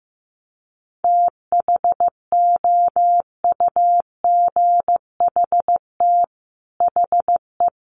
Morse code, heard as a keyed tone, about 15 wpm, 700 Hz. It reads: THOUGHT HE
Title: Sets of 5 Words Encore2 15wpm